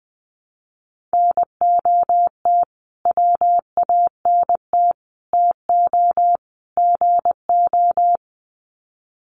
Morse code, heard as a keyed tone, about 20 wpm, 700 Hz. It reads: NOT WANT TO GO
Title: Sets of 4 Words 20wpm